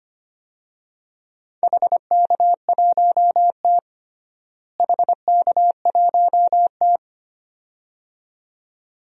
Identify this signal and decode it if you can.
Morse code, heard as a keyed tone, about 25 wpm, 700 Hz. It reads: HK1T HK1T
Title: DXCC-100 Call Signs - 4 Characters - Repeated 2x - 25wpm